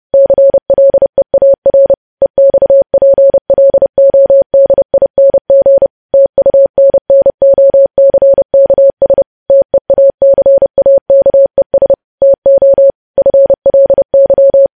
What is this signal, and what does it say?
Morse code, heard as a keyed tone, about 30 wpm, 550 Hz. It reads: CLEAR EXPLODING TUNNOCKS TEACAKES TO FLY
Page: news headlines in morse code at 30 wpm